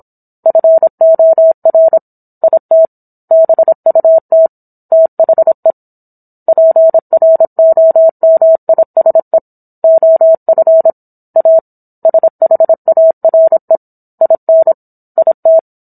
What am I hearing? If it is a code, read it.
Morse code, heard as a keyed tone, about 27 wpm, 650 Hz. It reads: FOR IT BUT THE PROMISE OF A SHARE IN IT